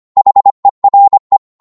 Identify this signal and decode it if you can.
Morse code, heard as a keyed tone, about 25 wpm, 800 Hz. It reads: HERE